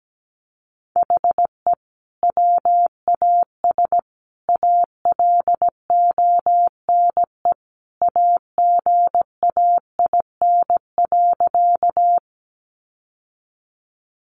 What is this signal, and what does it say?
Morse code, heard as a keyed tone, about 17 wpm, 700 Hz. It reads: HE WAS ALONE AGAIN.
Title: Sentences from Top 1000 Words 17wpm